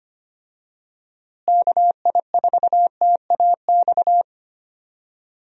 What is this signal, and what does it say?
Morse code, heard as a keyed tone, about 25 wpm, 700 Hz. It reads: KI4TAX
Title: USA Call Signs - 6 Characters - Repeated 3x - 25wpm